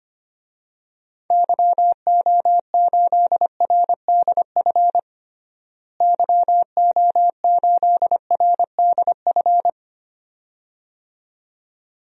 Morse code, heard as a keyed tone, about 25 wpm, 700 Hz. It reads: YO8RDF YO8RDF
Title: DXCC-100 Call Signs - 6 Characters - Repeated 2x - 25wpm